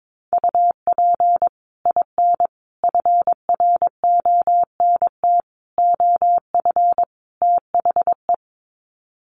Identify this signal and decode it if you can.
Morse code, heard as a keyed tone, about 22 wpm, 700 Hz. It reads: UP IN FRONT OF THE
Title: Sets of 5 Words Encore2 22wpm